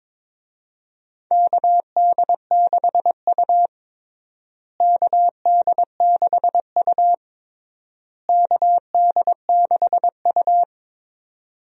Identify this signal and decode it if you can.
Morse code, heard as a keyed tone, about 22 wpm, 700 Hz. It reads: KD6U KD6U KD6U
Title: USA Call Signs - 4 Characters - Repeated 3x - 22wpm